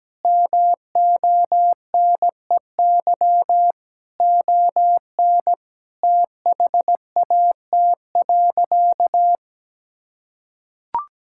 Morse code, heard as a keyed tone, about 17 wpm, 700 Hz. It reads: MONEY ON THAT.
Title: Sentences from Top 1000 Words - Review Part 5 - 17wpm